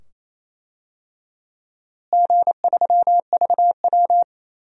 Morse code, heard as a keyed tone, about 28 wpm, 700 Hz. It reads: G3VW
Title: DXCC-100 Call Signs - 4 Characters - Repeated 2x - 28wpm